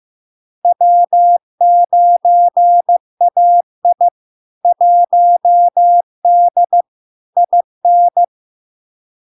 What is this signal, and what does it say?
Morse code, heard as a keyed tone, about 15 wpm, 700 Hz. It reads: W9AI 1D IN
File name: LARK-FD-2022-Log-in-CW-15wpm-700-Hz